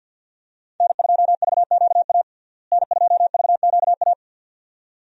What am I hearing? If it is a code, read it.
Morse code, heard as a keyed tone, about 52 wpm, 700 Hz. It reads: N2VQA N2VQA
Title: USA Call Signs - 5 Characters - Repeated 2x - 50wpm